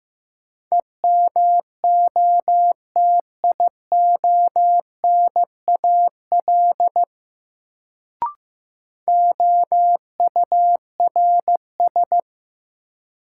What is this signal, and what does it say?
Morse code, heard as a keyed tone, about 15 wpm, 700 Hz. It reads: EMOTIONAL OURS
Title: Sentences from Top 1000 Words - Review Part 5 - New Word Drill - 15wpm